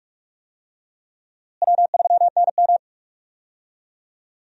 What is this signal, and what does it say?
Morse code, heard as a keyed tone, about 46 wpm, 700 Hz. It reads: W3NM
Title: USA Call Signs - 4 Characters - Repeated 3x - 45wpm